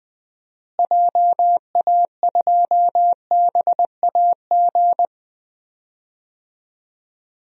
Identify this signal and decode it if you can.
Morse code, heard as a keyed tone, about 20 wpm, 700 Hz. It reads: JA2BAG
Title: DXCC-100 Call Signs - 6 Characters - Repeated 3x - 20wpm